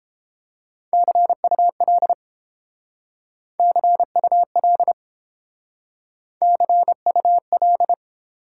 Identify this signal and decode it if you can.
Morse code, heard as a keyed tone, about 30 wpm, 700 Hz. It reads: CUL CUL CUL